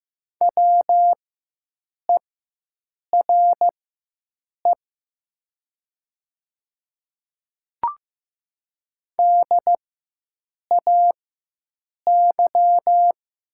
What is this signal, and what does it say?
Morse code, heard as a keyed tone, about 15 wpm, spaced as for 7 wpm, 700 Hz. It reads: WERE DAY
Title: 4x Character Spacing - Top 100 Words - 15wpm